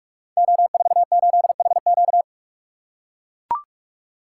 Morse code, heard as a keyed tone, about 44 wpm, 700 Hz. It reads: O48HQ
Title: Code Groups - 5 Letter:Numbers - 45wpm